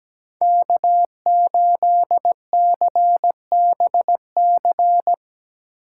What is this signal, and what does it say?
Morse code, heard as a keyed tone, about 20 wpm, 700 Hz. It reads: K8CBC
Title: Speed Racing - USA Call Signs 17wpm